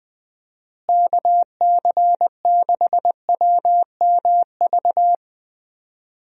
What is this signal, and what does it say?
Morse code, heard as a keyed tone, about 20 wpm, 700 Hz. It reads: KC6WMV
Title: USA Call Signs - 6 Characters - Repeated 3x - 20wpm